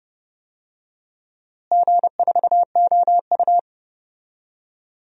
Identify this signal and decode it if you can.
Morse code, heard as a keyed tone, about 31 wpm, 700 Hz. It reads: G4OU
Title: DXCC-100 Call Signs - 4 Characters - Repeated 3x - 30wpm